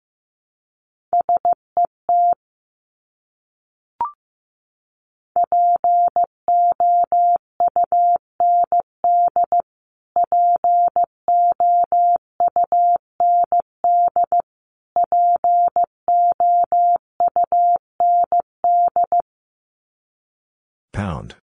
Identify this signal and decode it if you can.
Morse code, heard as a keyed tone, about 15 wpm, 700 Hz. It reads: SET POUND POUND POUND
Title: Top 500 Words Repeated 15wpm